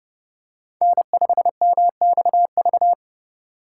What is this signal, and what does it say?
Morse code, heard as a keyed tone, about 30 wpm, 700 Hz. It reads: N5MXV